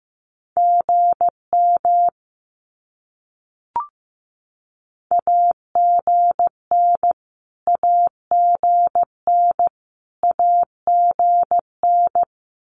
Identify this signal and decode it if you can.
Morse code, heard as a keyed tone, about 15 wpm, 700 Hz. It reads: GM AGN AGN AGN
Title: Abbreviations Repeated 3x - 15wpm